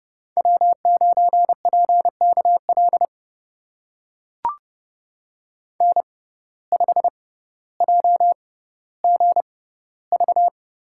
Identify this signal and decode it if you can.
Morse code, heard as a keyed tone, about 29 wpm, spaced as for 15 wpm, 700 Hz. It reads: W9PKL N5JGV
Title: USA Call Signs - 6x Character Spacing - 30wpm